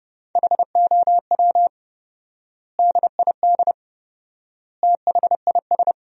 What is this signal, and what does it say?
Morse code, heard as a keyed tone, about 29 wpm, 700 Hz. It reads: HOW DID THIS
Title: 4x Word-Spacing - Sentences from Top 400 Words - 30wpm